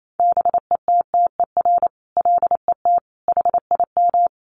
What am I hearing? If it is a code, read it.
Morse code, heard as a keyed tone, about 28 wpm, 700 Hz. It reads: BETTER LET HIM
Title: Sentences from Top 1200 Words 28wpm